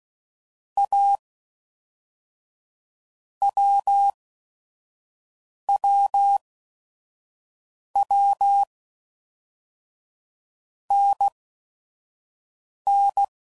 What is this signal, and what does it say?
Morse code, heard as a keyed tone, about 16 wpm, 800 Hz. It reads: A W W W N N
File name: Basic Letter Groups By Blocks / Groups@5 / AWUNDG@5 f15